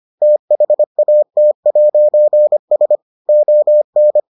Morse code, heard as a keyed tone, about 25 wpm, 600 Hz. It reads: THAT'S ON